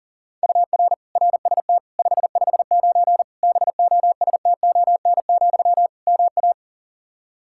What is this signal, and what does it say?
Morse code, heard as a keyed tone, about 40 wpm, 700 Hz. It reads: UR RST 559 BOSTON, MA
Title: K1USN SST QSO Practice 40wpm